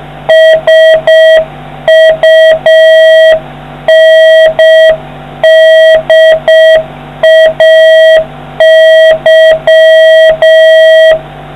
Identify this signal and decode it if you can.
Morse code, heard as a keyed tone, about 6 wpm, 1950 Hz, with noise. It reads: SUNDAY